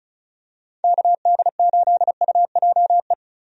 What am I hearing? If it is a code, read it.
Morse code, heard as a keyed tone, about 36 wpm, 700 Hz. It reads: KD8UJE
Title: USA Call Signs - 6 Characters - 35wpm